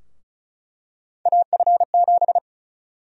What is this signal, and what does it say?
Morse code, heard as a keyed tone, about 34 wpm, 700 Hz. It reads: AF7
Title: USA Call Sign Prefixes - 35wpm